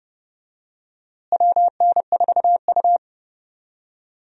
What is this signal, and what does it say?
Morse code, heard as a keyed tone, about 32 wpm, 700 Hz. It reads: WN4U
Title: USA Call Signs - 4 Characters - Repeated 2x - 30wpm